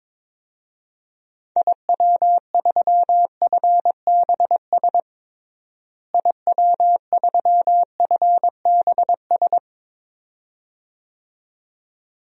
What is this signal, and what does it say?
Morse code, heard as a keyed tone, about 22 wpm, 700 Hz. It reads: IW3FBS IW3FBS
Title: DXCC-100 Call Signs - 6 Characters - Repeated 2x - 22wpm